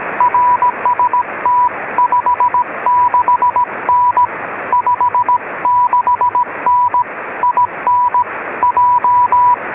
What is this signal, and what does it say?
Morse code, heard as a keyed tone, about 17 wpm, 980 Hz, with radio noise. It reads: RST 56N 56N IN J